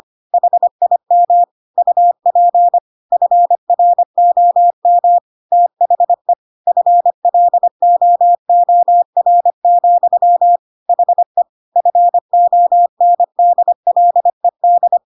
Morse code, heard as a keyed tone, about 25 wpm, 700 Hz. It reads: HIM UP FROM THE FLOOR, HE FONDLED
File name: Pinocchio / Pinocchio ch7 25 25 ik1hge